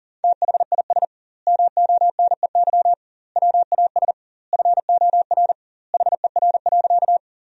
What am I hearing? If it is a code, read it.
Morse code, heard as a keyed tone, about 40 wpm, 700 Hz. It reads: THIS MONEY WAS FOR HER.
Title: 2x Word-Spacing - Sentences from Top 400 Words - 40wpm